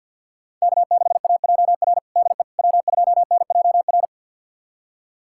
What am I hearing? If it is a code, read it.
Morse code, heard as a keyed tone, about 50 wpm, 700 Hz. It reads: K6AJR DE W2NJR
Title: USA Callsign DE Callsign - 5 Characters - 50wpm